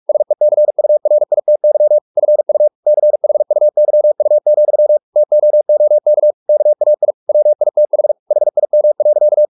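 Morse code, heard as a keyed tone, about 44 wpm, 600 Hz. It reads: SECURITY FU CHUYU, TOOK KAI WITH HIM.
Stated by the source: Morse practice at 45 wpm, the novel Never by Ken Follett in English, from CW med Gnister.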